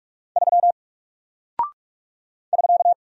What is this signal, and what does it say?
Morse code, heard as a keyed tone, about 46 wpm, 700 Hz. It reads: 3 <SK>
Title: Single Character (Letter:Number:Punctuation:Prosign) Alt 45wpm